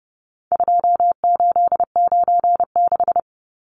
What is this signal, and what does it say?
Morse code, heard as a keyed tone, about 30 wpm, 700 Hz. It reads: 2896
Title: Numbers - 4-Digit 30wpm